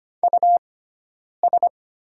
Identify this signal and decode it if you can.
Morse code, heard as a keyed tone, about 25 wpm, spaced as for 9 wpm, 700 Hz. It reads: US
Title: 6x Character Spacing - Top 200 Words - 25wpm